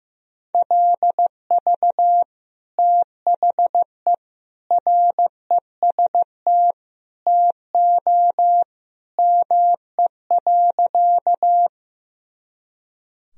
Morse code, heard as a keyed tone, about 15 wpm, 700 Hz. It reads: LV THE REST TO ME.
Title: Sentences from Top 500 Words and Abbreviations 15wpm